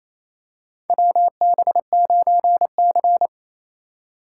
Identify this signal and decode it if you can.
Morse code, heard as a keyed tone, about 28 wpm, 700 Hz. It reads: WB9C
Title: USA Call Signs - 4 Characters - 4x Character Spacing - 28wpm